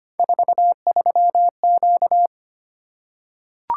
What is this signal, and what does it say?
Morse code, heard as a keyed tone, about 25 wpm, 700 Hz. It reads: 43Q
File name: Three Letter-Number Combinations - 25wpm